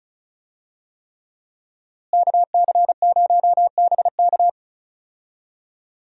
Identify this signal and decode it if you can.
Morse code, heard as a keyed tone, about 36 wpm, 700 Hz. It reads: KC0BK
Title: USA Call Signs - 5 Characters - Repeated 3x - 35wpm